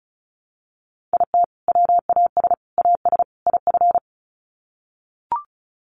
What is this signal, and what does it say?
Morse code, heard as a keyed tone, about 34 wpm, 700 Hz. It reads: IT WAS AS IF
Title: Sets of 4 Words 35wpm